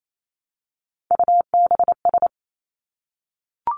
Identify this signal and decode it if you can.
Morse code, heard as a keyed tone, about 28 wpm, 700 Hz. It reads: UBS
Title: Three Challenging Letter-Number Combinations 28wpm